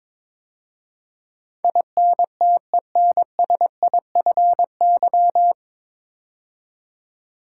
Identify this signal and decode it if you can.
Morse code, heard as a keyed tone, about 22 wpm, 700 Hz. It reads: INTENSIFY
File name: 9 Letter Words - First 3 Letters Given - 22wpm